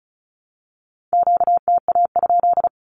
Morse code, heard as a keyed tone, about 34 wpm, 700 Hz. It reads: QTA?
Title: Q-Codes 35wpm